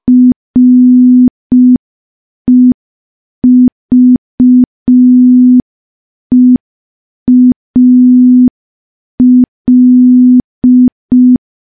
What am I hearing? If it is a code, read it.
Morse code, heard as a keyed tone, about 5 wpm, 250 Hz. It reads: REVEAL
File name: morse / morse-training-05WPM